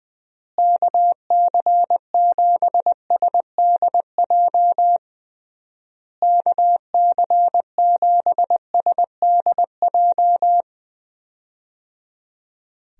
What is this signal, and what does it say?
Morse code, heard as a keyed tone, about 21 wpm, 700 Hz. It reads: KC7SDJ KC7SDJ
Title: USA Call Signs - 6 Characters - Repeated 3x - 20wpm